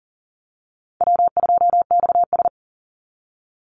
Morse code, heard as a keyed tone, about 40 wpm, 700 Hz. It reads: W2XS
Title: Easy Call Signs - 40wpm